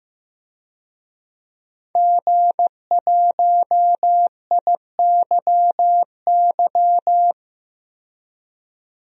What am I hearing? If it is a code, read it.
Morse code, heard as a keyed tone, about 15 wpm, 700 Hz. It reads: G1IYY
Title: DXCC-100 Call Signs - 5 Characters - 15wpm